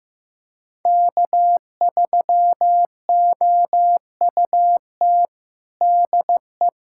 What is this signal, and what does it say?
Morse code, heard as a keyed tone, about 16 wpm, 700 Hz. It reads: K3OUT DE
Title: USA Callsign DE Callsign - 5 Characters - 15wpm